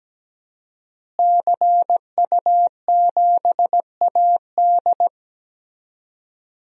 Morse code, heard as a keyed tone, about 17 wpm, 700 Hz. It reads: CU7AD
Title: DXCC-100 Call Signs - 5 Characters - Repeated 3x - 17wpm